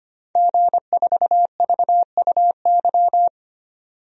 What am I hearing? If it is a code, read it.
Morse code, heard as a keyed tone, about 25 wpm, 700 Hz. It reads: G4VUY
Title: DXCC-100 Call Signs - 5 Characters - 25wpm